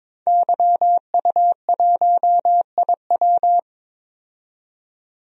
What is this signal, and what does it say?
Morse code, heard as a keyed tone, about 23 wpm, 700 Hz. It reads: YU1IW
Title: DXCC-100 Call Signs - 5 Characters - 22wpm